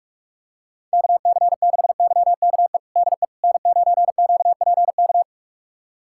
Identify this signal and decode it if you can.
Morse code, heard as a keyed tone, about 46 wpm, 700 Hz. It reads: KC6YKE DE N9QPK
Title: USA Callsign DE Callsign 45wpm